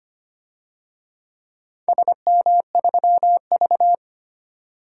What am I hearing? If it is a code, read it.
Morse code, heard as a keyed tone, about 25 wpm, 700 Hz. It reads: SM3V
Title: DXCC-100 Call Signs - 4 Characters - 25wpm